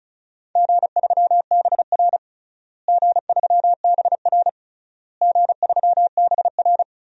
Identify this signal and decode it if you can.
Morse code, heard as a keyed tone, about 36 wpm, 700 Hz. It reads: G3BR G3BR G3BR
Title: DXCC-100 Call Signs - 4 Characters - Repeated 3x - 35wpm